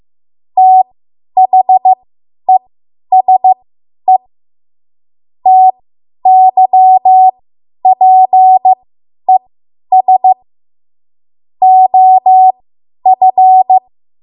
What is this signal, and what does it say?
Morse code, heard as a keyed tone, about 15 wpm, spaced as for 10 wpm, 750 Hz. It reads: THESE TYPES OF
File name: Morse / 10WPM